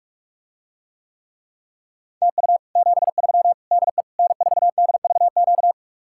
Morse code, heard as a keyed tone, about 49 wpm, 700 Hz. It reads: TU 73 DE N4DVQ